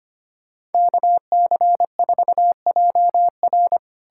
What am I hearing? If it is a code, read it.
Morse code, heard as a keyed tone, about 33 wpm, 700 Hz. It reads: KC4JR